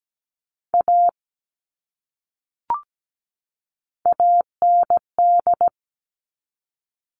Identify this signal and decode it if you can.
Morse code, heard as a keyed tone, about 17 wpm, 700 Hz. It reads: A AND
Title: Top 500 Words 17wpm